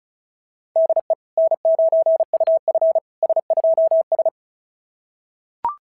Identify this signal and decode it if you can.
Morse code, heard as a keyed tone, about 36 wpm, 690 Hz. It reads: DE N9UF S2S
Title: SOTA QSO Practice 35wpm